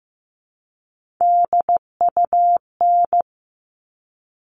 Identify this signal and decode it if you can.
Morse code, heard as a keyed tone, about 15 wpm, 700 Hz. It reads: DUN
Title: All Cut Numbers - 3-Digit 15wpm